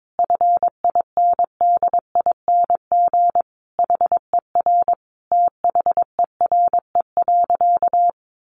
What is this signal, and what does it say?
Morse code, heard as a keyed tone, about 22 wpm, 700 Hz. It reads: FINDING HER THERE.
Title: Sentences from Top 1500 Words 22wpm